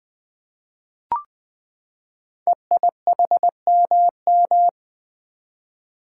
Morse code, heard as a keyed tone, about 20 wpm, 700 Hz. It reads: EIHMM
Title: Code Groups - 5 Letters - 20wpm